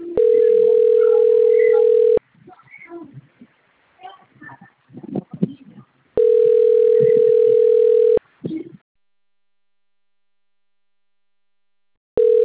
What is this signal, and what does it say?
Ringback tone (North America), 440 + 480 Hz.